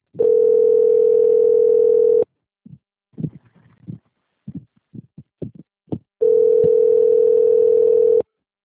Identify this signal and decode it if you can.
Ringback tone (North America), 440 + 480 Hz.
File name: out- - - - - .8137